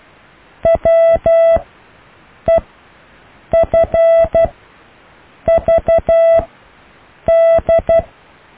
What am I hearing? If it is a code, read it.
Morse code, heard as a keyed tone, about 12 wpm, spaced as for 7 wpm, 660 Hz. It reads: WEFVD
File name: cw / EKTPV+BFHNW